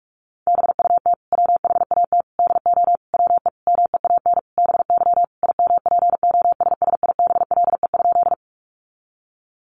Morse code, heard as a keyed tone, about 44 wpm, 700 Hz. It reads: BUT WHAT DO WE MEAN BY IMPOSSIBLE?
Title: Sentences from Top 1100 Words 45wpm